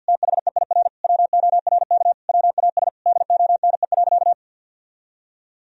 Morse code, heard as a keyed tone, about 51 wpm, 700 Hz. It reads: THEIR WORK WAS DONE.
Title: Sentences from Top 1000 Words - Review Part 1 - 50wpm